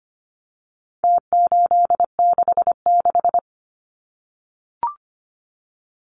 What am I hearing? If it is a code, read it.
Morse code, heard as a keyed tone, about 25 wpm, 700 Hz. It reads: T866
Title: Numbers - Contest Style - 4-Digit 25wpm